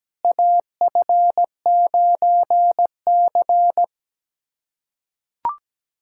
Morse code, heard as a keyed tone, about 17 wpm, 700 Hz. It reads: AF9C
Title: USA Callsign DE Callsign - 4 Characters - 17wpm